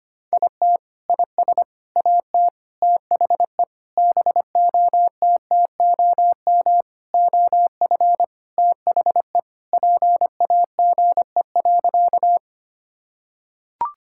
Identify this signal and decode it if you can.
Morse code, heard as a keyed tone, about 25 wpm, 700 Hz. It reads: IT IS AT THE BOTTOM OF THE PAGE.
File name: practice / 25wpm / Sentences with 4-Word Phrases - 25wpm